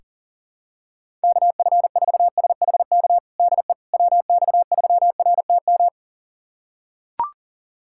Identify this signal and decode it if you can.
Morse code, heard as a keyed tone, about 40 wpm, 700 Hz. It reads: KF4SHK DE WX3RTM
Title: USA Callsign DE Callsign - 6 Characters - 40wpm